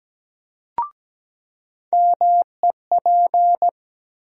Morse code, heard as a keyed tone, about 17 wpm, 700 Hz. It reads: MEP
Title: USA Call Sign Postfixes - 3 Letters 17wpm